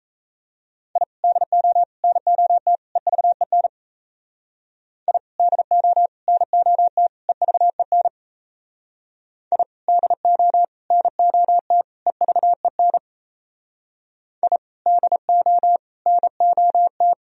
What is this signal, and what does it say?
Morse code, heard as a keyed tone, about 35 wpm, 700 Hz. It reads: I DO NOT EVEN I DO NOT EVEN I DO NOT EVEN I DO NOT
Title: Speed-Racing - Sets of 4 Words 28wpm